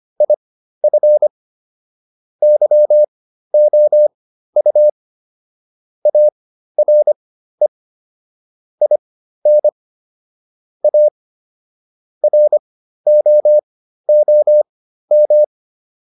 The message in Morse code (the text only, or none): IF YOU ARE IN A ROOM